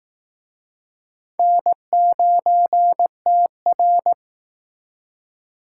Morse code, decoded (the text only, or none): N9TR